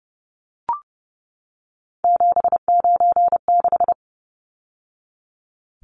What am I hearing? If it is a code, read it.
Morse code, heard as a keyed tone, about 30 wpm, 700 Hz. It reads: 796
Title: Numbers - 3-Digit 30wpm